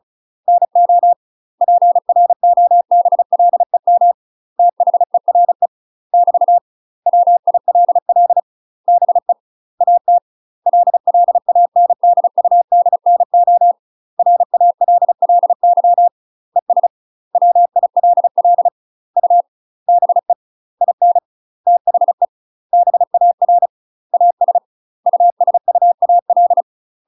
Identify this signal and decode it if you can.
Morse code, heard as a keyed tone, about 36 wpm, 700 Hz. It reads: NO PROBLEM THERE <BT> WILL BE AT LLANDUDNO RALLY ES WILL U BE IN THE BAR AS USUAL
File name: UK MASTER QSOs TRAINING / 35WPM / UK MASTER QSOs 0016